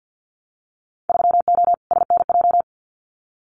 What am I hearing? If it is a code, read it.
Morse code, heard as a keyed tone, about 50 wpm, 700 Hz. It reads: 3O SNJ
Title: Winter Field Day Exchanges Encore 50wpm